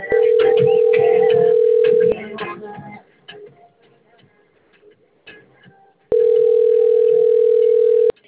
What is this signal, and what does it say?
Ringback tone (North America), 440 + 480 Hz.
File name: out- - - - - .39163